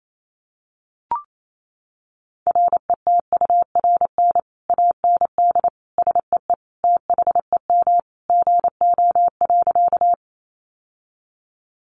Morse code, heard as a keyed tone, about 28 wpm, 700 Hz. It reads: RETURN AND SEE THEM GO.
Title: Sentences from Top 800 Words 28wpm